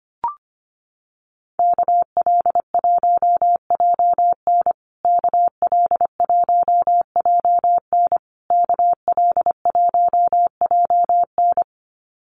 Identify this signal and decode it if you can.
Morse code, heard as a keyed tone, about 25 wpm, 700 Hz. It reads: KL1JN KL1JN KL1JN
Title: USA Call Signs Repeated 3x 25wpm